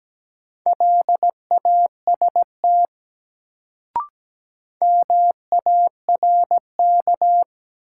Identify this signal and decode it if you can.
Morse code, heard as a keyed tone, about 17 wpm, 700 Hz. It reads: LAST MARK
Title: Sentences from Top 300 Words - New Word Drill - 17wpm